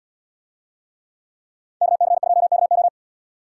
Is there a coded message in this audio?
Yes: Morse code from 1.8 to 2.9 s, heard as a keyed tone, about 77 wpm, 700 Hz. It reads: KC2KO